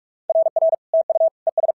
Morse code, heard as a keyed tone, about 46 wpm, 650 Hz. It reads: RR TU ES
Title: SOTA QSO Practice 45wpm